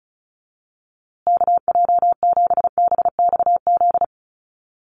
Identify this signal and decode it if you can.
Morse code, heard as a keyed tone, about 34 wpm, 700 Hz. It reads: KJ7BXZ
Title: USA Call Signs Repeated 3x 35wpm